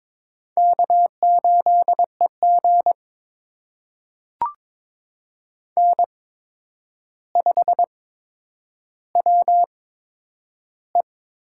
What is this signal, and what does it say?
Morse code, heard as a keyed tone, about 21 wpm, spaced as for 9 wpm, 700 Hz. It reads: K8EG N5WE
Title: USA Call Signs - 4 Characters - 8x Character Spacing - 22wpm